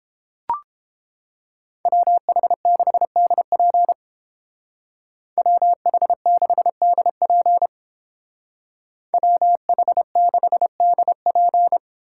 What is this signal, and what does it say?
Morse code, heard as a keyed tone, about 30 wpm, 700 Hz. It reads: WH6DP WH6DP WH6DP